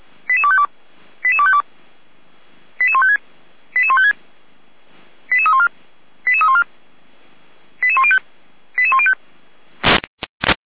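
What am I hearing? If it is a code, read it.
Five-tone selective call (ZVEI), heard as quick runs of five tones, tones of about 70 ms. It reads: 88252 88252 88156 88156 88314 88314 88105 88105
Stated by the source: ZVEI five-tone call received by SDR.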